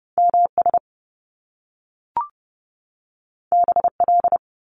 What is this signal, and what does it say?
Morse code, heard as a keyed tone, about 30 wpm, 700 Hz. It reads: MS BL
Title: Top Two Letter Combinations in Words 30wpm